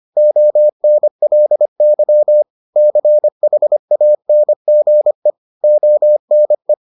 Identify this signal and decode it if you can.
Morse code, heard as a keyed tone, about 25 wpm, 600 Hz. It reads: ONLY CHANGE ONE